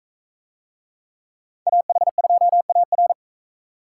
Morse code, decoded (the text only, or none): AH2AR